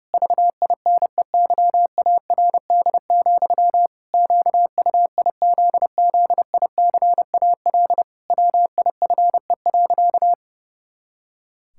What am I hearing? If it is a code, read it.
Morse code, heard as a keyed tone, about 30 wpm, 700 Hz. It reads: VINEYARD, QUIZZICAL WIFE.
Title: Pangram Challenge v2 - 30wpm